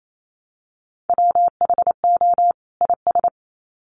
Morse code, heard as a keyed tone, about 28 wpm, 700 Hz. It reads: WHO IS
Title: Sets of 2 Words 28wpm